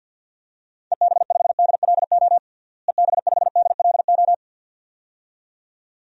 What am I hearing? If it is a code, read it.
Morse code, heard as a keyed tone, about 50 wpm, 700 Hz. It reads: EB5DLO EB5DLO
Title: DXCC-100 Call Signs - 6 Characters - Repeated 2x - 50wpm